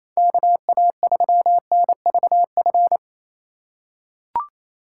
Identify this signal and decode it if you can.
Morse code, heard as a keyed tone, about 28 wpm, 700 Hz. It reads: KA3NVF
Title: USA Call Signs - 6 Characters - 3x Character Spacing - 28wpm